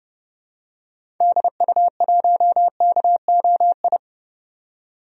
Morse code, heard as a keyed tone, about 32 wpm, 700 Hz. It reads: DU1KOI
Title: DXCC-100 Call Signs - 6 Characters - 30wpm